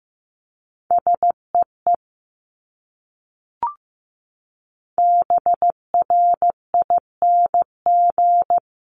Morse code, heard as a keyed tone, about 15 wpm, 700 Hz. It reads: SEE BRING
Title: Top 500 Words Repeated 15wpm